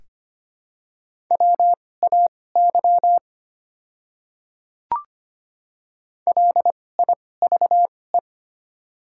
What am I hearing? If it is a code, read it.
Morse code, heard as a keyed tone, about 25 wpm, spaced as for 18 wpm, 700 Hz. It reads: WAY LIVE
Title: 2x Character Spacing - Top 200 Words - 25wpm